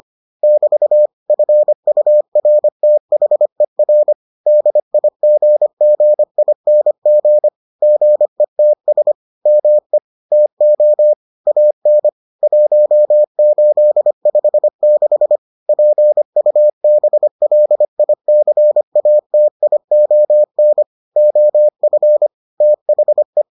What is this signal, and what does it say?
Morse code, heard as a keyed tone, about 25 wpm, 600 Hz. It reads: <BT> FURTHER DIGGING GETS ME TO AN 1856 PUBLICATION OF THE